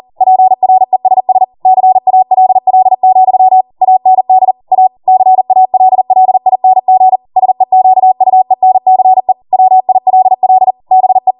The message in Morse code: PRESS CALL, AND A CALLING SEQUENCE WILL BE